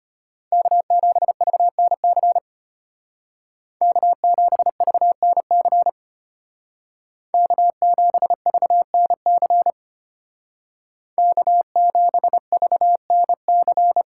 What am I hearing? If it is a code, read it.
Morse code, heard as a keyed tone, about 30 wpm, 700 Hz. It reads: K7VNC K7VNC K7VNC K7VNC
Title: Speed Racing - USA 1x3 Call Signs 25wpm